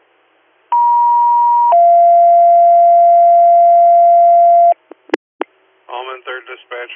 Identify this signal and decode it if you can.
Two-tone page, heard as a one-second tone, then a three-second tone, about 950 then 690 Hz, with speech. It is heard from 0.7 to 4.7 s.